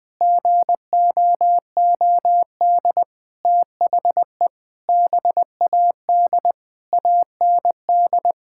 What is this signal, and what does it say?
Morse code, heard as a keyed tone, about 20 wpm, 700 Hz. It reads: GOOD THE BAD AND
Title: Sentences with 6-Word Phrases - Spoken Beginning - 20wpm